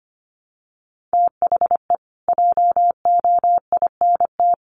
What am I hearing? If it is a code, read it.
Morse code, heard as a keyed tone, about 25 wpm, 700 Hz. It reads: THE JOINT
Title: Sets of 5 Words Encore2 25wpm